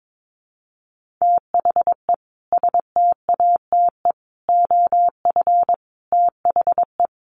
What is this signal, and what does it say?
Morse code, heard as a keyed tone, about 22 wpm, 700 Hz. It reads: THE STATE OF THE
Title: Sets of 5 Words Encore 22wpm